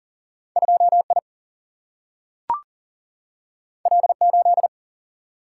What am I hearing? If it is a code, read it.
Morse code, heard as a keyed tone, about 40 wpm, 700 Hz. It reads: 2 I L 8
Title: Single-Letter Single-Number 40wpm